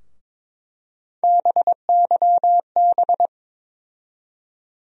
Morse code, heard as a keyed tone, about 22 wpm, 700 Hz. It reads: BYB
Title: USA Call Sign Postfixes - 3 Letters 22wpm